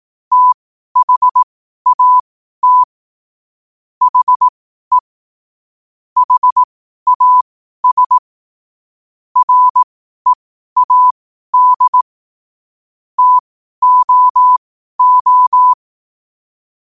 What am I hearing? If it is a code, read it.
Morse code, heard as a keyed tone, about 18 wpm, spaced as for 12 wpm, 1000 Hz. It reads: THAT HE HAS READ TOO
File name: qotdfiles / 12wpm